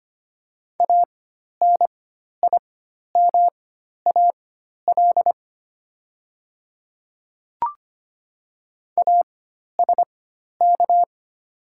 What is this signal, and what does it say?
Morse code, heard as a keyed tone, about 25 wpm, spaced as for 12 wpm, 700 Hz. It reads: ANIMAL ASK
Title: 4x Character Spacing - Top 500 Words - 25wpm